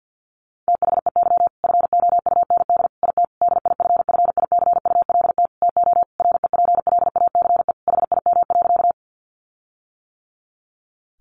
Morse code, heard as a keyed tone, about 50 wpm, 700 Hz. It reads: THEY FOUND IT DIFFICULT TO REPLACE HIM.
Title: Sentences from Top 1400 Words 50wpm